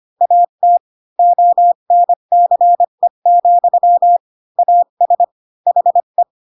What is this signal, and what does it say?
Morse code, heard as a keyed tone, about 25 wpm, 700 Hz. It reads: AT ONCE, AS HE
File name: Pinocchio / Pinocchio ch8 20 25 ik1hge